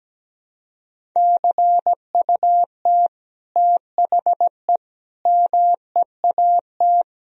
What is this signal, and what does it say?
Morse code, heard as a keyed tone, about 17 wpm, 700 Hz. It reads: CUT THE MEAT
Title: Sentences from Top 1000 Words - Review Part 3 - 17wpm